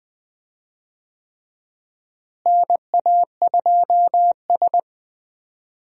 Morse code, heard as a keyed tone, about 20 wpm, 700 Hz. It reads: NA2S
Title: USA Call Signs - 4 Characters - Repeated 3x - 20wpm